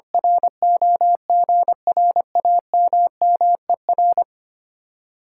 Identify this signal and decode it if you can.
Morse code, heard as a keyed tone, about 25 wpm, 700 Hz. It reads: ROGRAMMER
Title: Speed-Racing Common QSO Phrases - 25wpm